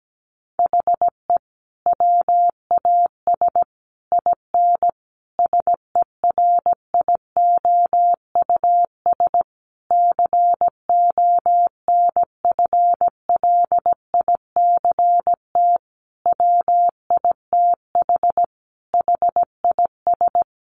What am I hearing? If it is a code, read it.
Morse code, heard as a keyed tone, about 17 wpm, 700 Hz. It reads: HE WAS IN SERIOUS CONFLICT WITH HIS
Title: Sentences from Top 1300 Words 17wpm